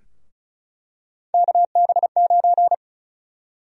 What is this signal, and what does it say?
Morse code, heard as a keyed tone, about 34 wpm, 700 Hz. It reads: KB9
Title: USA Call Sign Prefixes - 35wpm